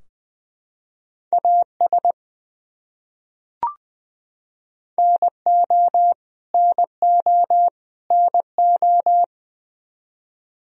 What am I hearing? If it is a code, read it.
Morse code, heard as a keyed tone, about 19 wpm, 700 Hz. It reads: AS NO NO NO
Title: Top 100 Words Repeated 20wpm